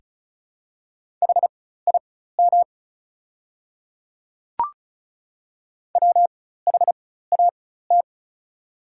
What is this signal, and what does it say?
Morse code, heard as a keyed tone, about 36 wpm, spaced as for 16 wpm, 690 Hz. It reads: HIM WHAT
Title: 4x Character Spacing - Top 100 Words - 35wpm